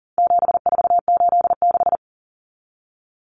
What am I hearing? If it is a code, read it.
Morse code, heard as a keyed tone, about 40 wpm, 700 Hz. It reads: 7486
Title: Common Cut Numbers - 4-Digit 40wpm